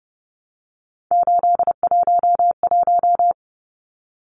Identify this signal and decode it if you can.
Morse code, heard as a keyed tone, about 30 wpm, 700 Hz. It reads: 811
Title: Numbers - 3-Digit 30wpm